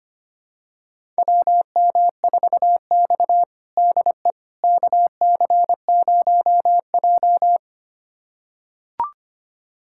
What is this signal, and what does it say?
Morse code, heard as a keyed tone, about 25 wpm, 700 Hz. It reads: WM4X DE KC0J
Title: USA Callsign DE Callsign - 4 Characters - 25wpm